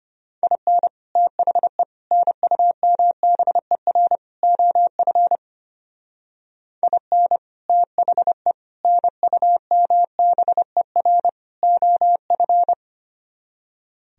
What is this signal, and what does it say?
Morse code, heard as a keyed tone, about 32 wpm, 700 Hz. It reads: IN THE NUMBER OF IN THE NUMBER OF